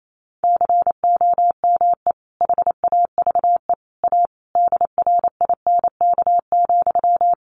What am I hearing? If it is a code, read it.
Morse code, heard as a keyed tone, about 28 wpm, 700 Hz. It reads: COME HAVE A DRINK,
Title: Sentences from Top 1500 Words 28wpm